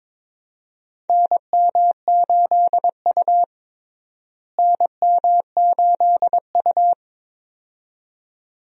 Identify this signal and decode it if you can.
Morse code, heard as a keyed tone, about 22 wpm, 700 Hz. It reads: NM8U NM8U
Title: USA Call Signs - 4 Characters - Repeated 2x - 22wpm